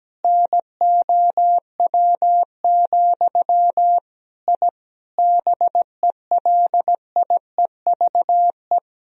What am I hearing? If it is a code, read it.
Morse code, heard as a keyed tone, about 17 wpm, 700 Hz. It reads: NOW, I BELIEVE
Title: Sentences from Top 1000 Words - Review Part 1 - 17wpm